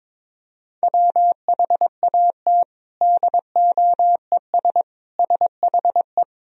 Morse code, heard as a keyed tone, about 22 wpm, 700 Hz. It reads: WHAT DOES SHE
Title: Sentences from Top 200 Words and Abbreviations 22wpm